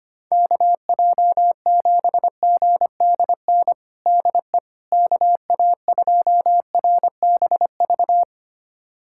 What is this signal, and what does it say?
Morse code, heard as a keyed tone, about 25 wpm, 700 Hz. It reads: KJ7GDN DE KA2RBV
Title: USA Callsign DE Callsign - 6 Characters - 25wpm